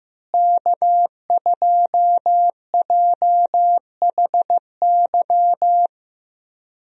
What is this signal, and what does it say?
Morse code, heard as a keyed tone, about 15 wpm, 700 Hz. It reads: K2JHY
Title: USA Call Signs - 5 Characters - Repeated 2x - 15wpm